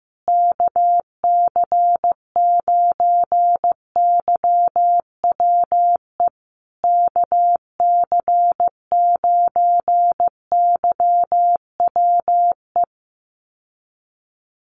Morse code, heard as a keyed tone, about 15 wpm, 700 Hz. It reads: KC9YWE KC9YWE
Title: USA Call Signs Repeated 3x 15wpm